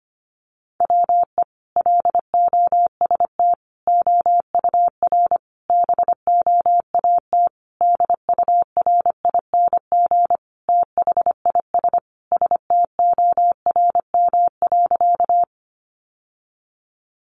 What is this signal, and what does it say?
Morse code, heard as a keyed tone, about 25 wpm, 700 Hz. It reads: WE LOST OUR BOAT DURING THIS STORM.
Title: Sentences from Top 1500 Words 25wpm